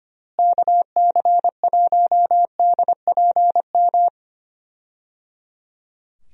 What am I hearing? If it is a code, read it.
Morse code, heard as a keyed tone, about 25 wpm, 700 Hz. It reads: KC1DPM
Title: USA Call Signs - 6 Characters - Repeated 3x - 25wpm